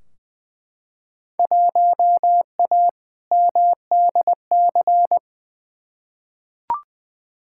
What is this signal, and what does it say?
Morse code, heard as a keyed tone, about 21 wpm, 700 Hz. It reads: 1A MDC
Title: Field Day Exchanges - 20wpm